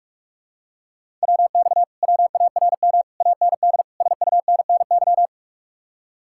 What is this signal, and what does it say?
Morse code, heard as a keyed tone, about 49 wpm, 700 Hz. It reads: WX WARM AND SUNNY